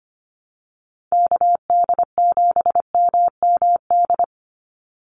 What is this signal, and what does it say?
Morse code, heard as a keyed tone, about 25 wpm, 700 Hz. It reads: KD7MMD
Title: USA Call Signs 25wpm